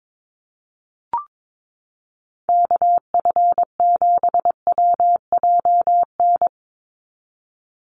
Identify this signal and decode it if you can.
Morse code, heard as a keyed tone, about 22 wpm, 700 Hz. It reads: KF7WJN
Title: USA Call Signs 22wpm